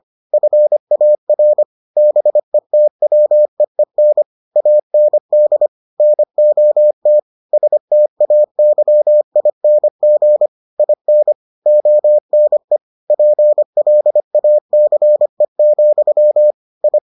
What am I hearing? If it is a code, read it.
Morse code, heard as a keyed tone, about 25 wpm, 600 Hz. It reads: FAR BETWEEN AND NOT STAYING IN ONE PLACE, I